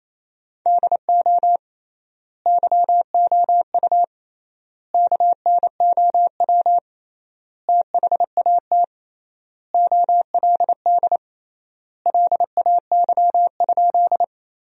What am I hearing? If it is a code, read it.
Morse code, heard as a keyed tone, about 28 wpm, 700 Hz. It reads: DO YOU KNOW THAT OLD LAY?
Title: 3x Word-Spacing - Sentences from Top 400 Words - 28wpm